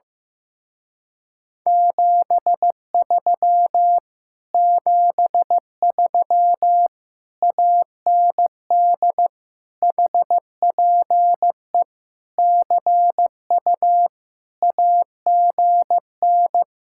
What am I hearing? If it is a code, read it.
Morse code, heard as a keyed tone, about 15 wpm, 700 Hz. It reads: 73 73 AND HPE CU AGN
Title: Speed-Racing Common QSO Phrases Encore - 15wpm